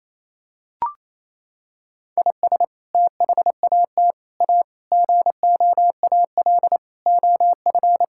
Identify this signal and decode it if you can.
Morse code, heard as a keyed tone, about 28 wpm, 700 Hz. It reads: IS THAT A GOAL OF
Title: Sentences from Top 1000 Words - Review Part 5 - 28wpm